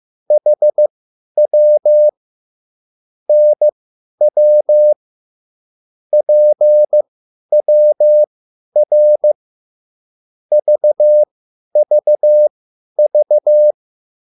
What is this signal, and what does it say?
Morse code, heard as a keyed tone, about 15 wpm, spaced as for 10 wpm, 600 Hz. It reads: HW NW PWR VVV
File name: QSL / CW lesson